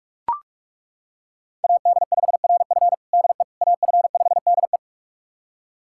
Morse code, heard as a keyed tone, about 41 wpm, 700 Hz. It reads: AD5RF DE AF5DE
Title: USA Callsign DE Callsign - 5 Characters - 45wpm